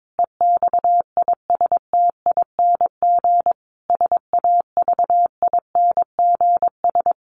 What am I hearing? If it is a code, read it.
Morse code, heard as a keyed tone, about 22 wpm, 700 Hz. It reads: EXISTING SAVINGS
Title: Sentences from Top 1500 Words 22wpm